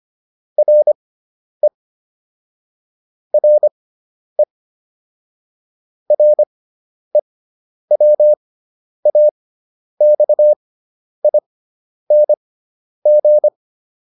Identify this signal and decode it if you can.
Morse code, heard as a keyed tone, about 25 wpm, spaced as for 10 wpm, 600 Hz. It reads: RE RE REWAXING